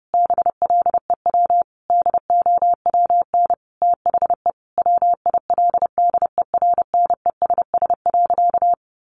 Morse code, heard as a keyed tone, about 28 wpm, 700 Hz. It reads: BLEW DOWN THE WILDERNESS.
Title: A Sound Of Thunder 30wpm